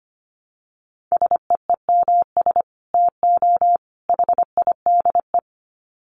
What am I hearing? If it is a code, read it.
Morse code, heard as a keyed tone, about 25 wpm, 700 Hz. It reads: SEEMS TO HIDE
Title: Dit Challenge 25wpm